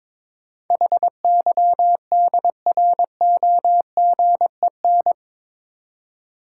Morse code, heard as a keyed tone, about 21 wpm, 700 Hz. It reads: HYDROGEN
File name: practice / 22wpm / Eight Letter Words - 22wpm